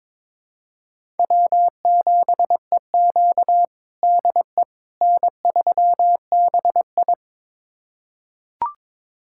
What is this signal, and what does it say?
Morse code, heard as a keyed tone, about 22 wpm, 700 Hz. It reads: W7EQ DE N3BI
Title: USA Callsign DE Callsign - 4 Characters - 22wpm